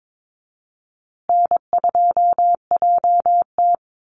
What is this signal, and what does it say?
Morse code, heard as a keyed tone, about 22 wpm, 700 Hz. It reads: N2JT
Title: Easy Call Signs Repeated 3x - 22wpm